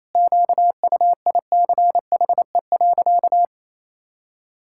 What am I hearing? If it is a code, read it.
Morse code, heard as a keyed tone, about 28 wpm, 700 Hz. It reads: QUICHE.
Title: Pangram Challenge v2 - 28wpm